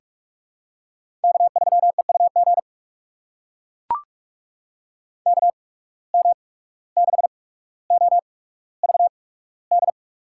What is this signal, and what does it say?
Morse code, heard as a keyed tone, about 44 wpm, spaced as for 18 wpm, 700 Hz. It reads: K3EUG KM6OVD
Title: USA Call Signs - 8x Character Spacing - 45wpm